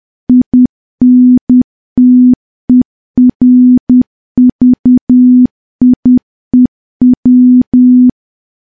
Morse code, heard as a keyed tone, about 10 wpm, 250 Hz. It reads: INTERVIEW
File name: morse / morse-training-10WPM